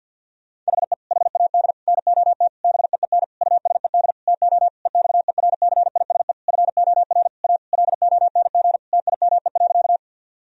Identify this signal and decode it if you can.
Morse code, heard as a keyed tone, about 50 wpm, 700 Hz. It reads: HE HAD NOT BEEN USED TO EXERCISE FOR A LONG TIME.
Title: Sentences from Top 1000 Words - Review Part 4 - 50wpm